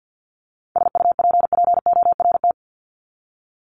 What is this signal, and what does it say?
Morse code, heard as a keyed tone, about 50 wpm, 700 Hz. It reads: SUPPORT
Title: Top 751-1000 Words 50wpm